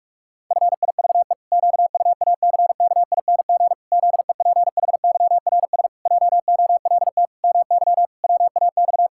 Morse code, heard as a keyed tone, about 46 wpm, 700 Hz. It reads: FIVE QUACKING ZEPHYRS JOLT MY WAX
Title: Pangram Challenge v2 - 45wpm